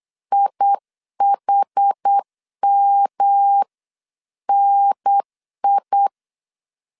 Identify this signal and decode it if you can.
Morse code, heard as a keyed tone, about 8 wpm, 800 Hz. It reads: IHMNI